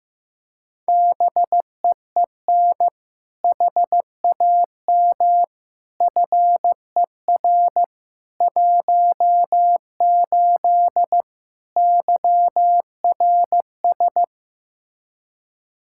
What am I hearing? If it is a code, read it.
Morse code, heard as a keyed tone, about 15 wpm, 700 Hz. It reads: BEEN HAM FER 18 YRS
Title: Practice QSO - Exchange 3 - More Info - 15wpm